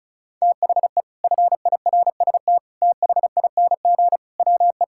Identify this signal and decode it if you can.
Morse code, heard as a keyed tone, about 36 wpm, 700 Hz. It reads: THE FIRST THING WE
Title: Sentences from Top 1000 Words - Review Part 6 - 35wpm